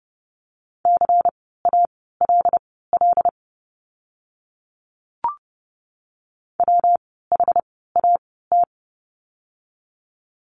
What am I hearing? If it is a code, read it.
Morse code, heard as a keyed tone, about 30 wpm, spaced as for 17 wpm, 700 Hz. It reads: CALL WHAT
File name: course / 30:10wpm / 230 Lesson 37f - Letter x - Words 30wpm